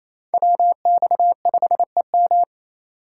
Morse code, heard as a keyed tone, about 36 wpm, 700 Hz. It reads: WX5EM